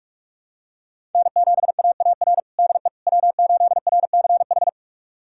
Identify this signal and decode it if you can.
Morse code, heard as a keyed tone, about 45 wpm, 700 Hz. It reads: N7AAR DE W8RCH
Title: USA Callsign DE Callsign - 5 Characters - 45wpm